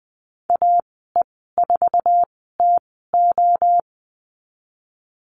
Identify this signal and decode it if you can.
Morse code, heard as a keyed tone, about 20 wpm, spaced as for 15 wpm, 700 Hz. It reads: AE4TO
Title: 055 Lesson 10g - Letter r - Call Signs 20wpm